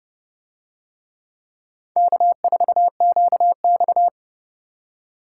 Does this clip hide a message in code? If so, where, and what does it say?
Yes: Morse code from 2.0 to 4.1 s, heard as a keyed tone, about 28 wpm, 700 Hz. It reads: K4QX